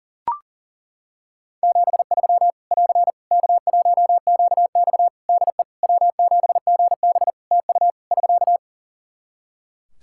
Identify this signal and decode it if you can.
Morse code, heard as a keyed tone, about 40 wpm, 700 Hz. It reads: 73 <AR> K1QX DE W7GB TU <SK>
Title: K1USN SST QSO Practice 40wpm